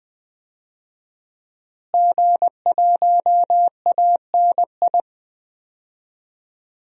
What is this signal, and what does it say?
Morse code, heard as a keyed tone, about 20 wpm, 700 Hz. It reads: G1ANI
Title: DXCC-100 Call Signs - 5 Characters - Repeated 2x - 20wpm